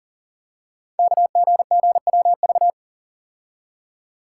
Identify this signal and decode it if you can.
Morse code, heard as a keyed tone, about 40 wpm, 690 Hz. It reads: KGGWV
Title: Code Groups - 5 Letters - 40wpm